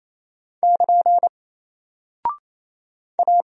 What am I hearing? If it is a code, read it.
Morse code, heard as a keyed tone, about 28 wpm, 700 Hz. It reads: <KN> A
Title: Single Character (Letter:Number:Punctuation:Prosign) Alt 28wpm